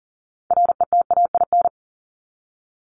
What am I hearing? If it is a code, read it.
Morse code, heard as a keyed tone, about 40 wpm, 700 Hz. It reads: RETAIN
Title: Top 1751-2000 Words 40wpm